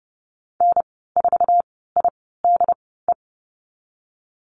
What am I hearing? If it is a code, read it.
Morse code, heard as a keyed tone, about 30 wpm, spaced as for 17 wpm, 700 Hz. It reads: N4IDE
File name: Lesson 14g - Number 2 - Call Signs 30wpm